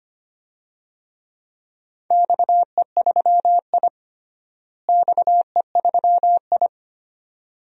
Morse code, heard as a keyed tone, about 25 wpm, 700 Hz. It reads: XE3I XE3I
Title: DXCC-100 Call Signs - 4 Characters - Repeated 2x - 25wpm